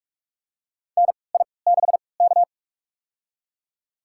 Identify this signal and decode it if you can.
Morse code, heard as a keyed tone, about 46 wpm, spaced as for 26 wpm, 690 Hz. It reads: NI6K
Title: USA Call Signs - 4 Characters - 3x Character Spacing - 45wpm